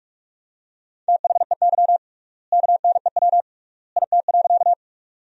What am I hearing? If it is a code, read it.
Morse code, heard as a keyed tone, about 41 wpm, 700 Hz. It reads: THEY KNEW IT.
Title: 3x Word-Spacing - Sentences from Top 400 Words - 45wpm